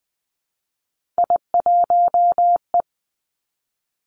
Morse code, heard as a keyed tone, about 20 wpm, 700 Hz. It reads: I1E
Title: Three Challenging Letter-Number Combinations 20wpm